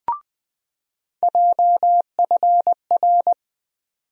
Morse code, heard as a keyed tone, about 21 wpm, 700 Hz. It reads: JFR
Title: USA Call Sign Postfixes - 3 Letters 20wpm